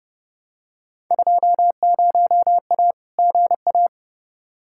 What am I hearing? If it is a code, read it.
Morse code, heard as a keyed tone, about 32 wpm, 700 Hz. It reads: 20A GA
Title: Field Day Exchanges Encore - 30wpm